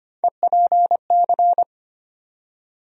Morse code, heard as a keyed tone, about 25 wpm, 700 Hz. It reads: EPC
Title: USA Call Sign Postfixes - 3 Letters 25wpm